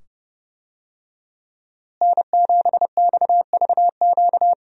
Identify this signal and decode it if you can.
Morse code, heard as a keyed tone, about 30 wpm, 700 Hz. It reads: N7XVQ
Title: USA Call Signs - 5 Characters - Repeated 2x - 30wpm